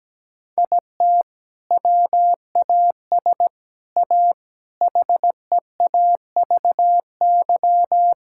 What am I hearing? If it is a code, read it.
Morse code, heard as a keyed tone, about 17 wpm, 700 Hz. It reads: IT WAS A HEAVY
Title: Sentences from Top 1600 Words 17wpm